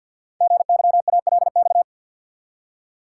Morse code, heard as a keyed tone, about 48 wpm, 700 Hz. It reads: GYALX
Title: Code Groups - 5 Letters - 50wpm